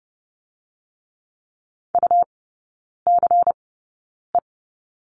Morse code, heard as a keyed tone, about 30 wpm, spaced as for 9 wpm, 700 Hz. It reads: UCE
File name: Lesson 17e - Letter u - Three Characters 30wpm